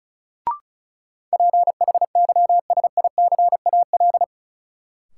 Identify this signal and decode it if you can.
Morse code, heard as a keyed tone, about 34 wpm, 700 Hz. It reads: PHYSICAL
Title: Sentences from Top 1000 Words - Review Part 4 - New Word Drill - 35wpm